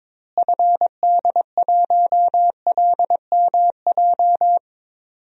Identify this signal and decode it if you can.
Morse code, heard as a keyed tone, about 22 wpm, 700 Hz. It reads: FD1LMJ
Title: DXCC-100 Call Signs - 6 Characters - Repeated 2x - 22wpm